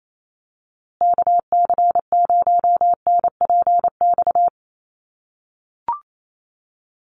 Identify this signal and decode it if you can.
Morse code, heard as a keyed tone, about 28 wpm, 700 Hz. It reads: KC0NPX
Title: USA Call Signs Repeated 2x 28wpm